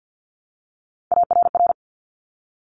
Morse code, heard as a keyed tone, about 50 wpm, 700 Hz. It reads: ARR
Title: Top Three Letter Combinations in Words 50wpm